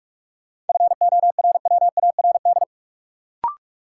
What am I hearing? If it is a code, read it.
Morse code, heard as a keyed tone, about 43 wpm, 700 Hz. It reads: FORWARD
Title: Sentences from Top 1300 Words - New Word Drill - 45wpm